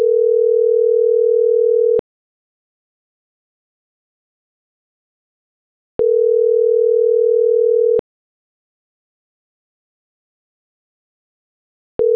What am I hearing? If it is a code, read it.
Ringback tone (North America), 440 + 480 Hz.